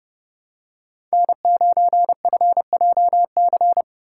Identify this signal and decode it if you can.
Morse code, heard as a keyed tone, about 29 wpm, 700 Hz. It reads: N9FJC